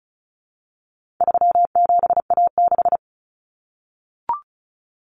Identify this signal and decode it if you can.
Morse code, heard as a keyed tone, about 34 wpm, 700 Hz. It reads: 37A6
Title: Common Cut Numbers - 4-Digit 35wpm